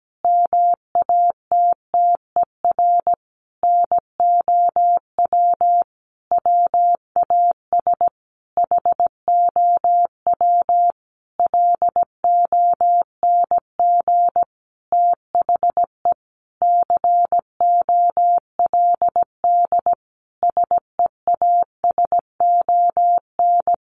Morse code, heard as a keyed tone, about 17 wpm, 700 Hz. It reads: MATTER NOW WAS HOW LONG THE COLD SEASON
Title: Sentences from Top 1400 Words 17wpm